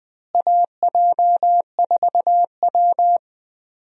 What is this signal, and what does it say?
Morse code, heard as a keyed tone, about 21 wpm, 700 Hz. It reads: AJ4W
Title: USA Call Signs - 4 Characters - Repeated 2x - 20wpm